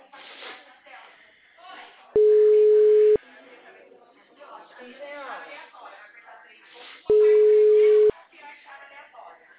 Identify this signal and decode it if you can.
Ringback tone (Europe, 425 Hz), with speech.
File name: out- -2009- - - .187